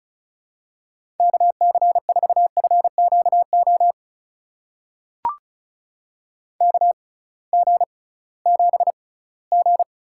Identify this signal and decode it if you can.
Morse code, heard as a keyed tone, about 36 wpm, spaced as for 21 wpm, 700 Hz. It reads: KC4FQO KG7G
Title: USA Call Signs - 6 Characters - 6x Character Spacing - 35wpm